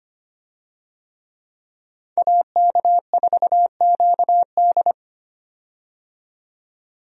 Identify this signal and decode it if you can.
Morse code, heard as a keyed tone, about 25 wpm, 700 Hz. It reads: AK4QD
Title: USA Call Signs - 5 Characters - 25wpm